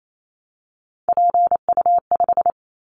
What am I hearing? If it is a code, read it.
Morse code, heard as a keyed tone, about 28 wpm, 700 Hz. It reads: PU5
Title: Callsign Prefixes for Top 100 DXCC - 28wpm